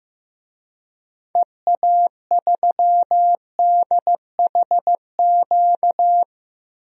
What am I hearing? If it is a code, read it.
Morse code, heard as a keyed tone, about 15 wpm, 700 Hz. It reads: EA3DHQ
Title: DXCC-100 Call Signs - 6 Characters - 15wpm